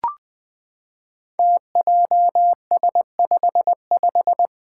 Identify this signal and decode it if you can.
Morse code, heard as a keyed tone, about 20 wpm, 700 Hz. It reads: TJS55
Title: Five Challenging Letter-Number Combinations 20wpm